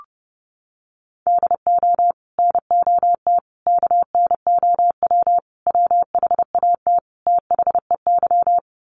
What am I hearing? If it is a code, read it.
Morse code, heard as a keyed tone, about 30 wpm, 700 Hz. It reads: DO NOT KNOW WHAT THEY
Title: Sets of 5 Words 30wpm